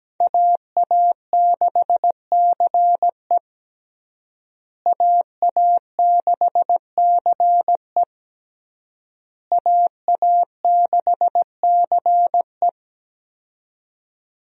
Morse code, heard as a keyed tone, about 17 wpm, 700 Hz. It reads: AA6CE AA6CE AA6CE
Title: USA Call Signs - 5 Characters - Repeated 3x - 17wpm